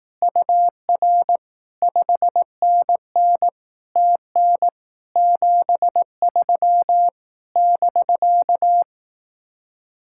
Morse code, heard as a keyed tone, about 18 wpm, 700 Hz. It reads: UR 5NN TN 73 <BK>
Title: Speed-Racing Common QSO Phrases Encore - 15wpm